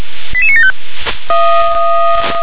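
Five-tone selective call (ZVEI), heard as quick runs of five tones, tones of about 70 ms. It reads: 88975
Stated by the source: ZVEI five-tone call received by SDR.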